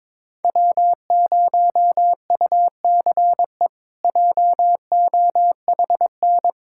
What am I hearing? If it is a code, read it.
Morse code, heard as a keyed tone, about 22 wpm, 700 Hz. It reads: W0UCE JOHN
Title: K1USN SST QSO Practice - 22wpm